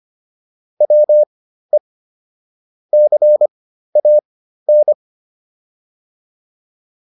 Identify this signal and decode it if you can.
Morse code, heard as a keyed tone, about 25 wpm, spaced as for 13 wpm, 600 Hz. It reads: WE CAN